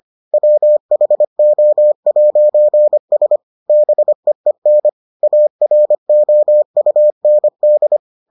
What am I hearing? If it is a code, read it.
Morse code, heard as a keyed tone, about 25 wpm, 600 Hz. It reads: WHO'S BEEN AROUND